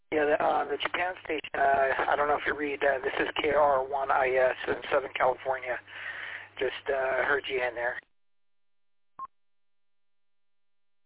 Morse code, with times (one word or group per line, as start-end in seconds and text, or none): none